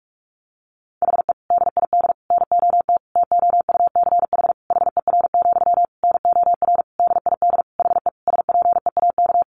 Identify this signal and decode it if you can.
Morse code, heard as a keyed tone, about 44 wpm, 700 Hz. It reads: HE DID NOT TOUCH HER, NOR DID HE SPEAK
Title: Sentences from Top 1500 Words 45wpm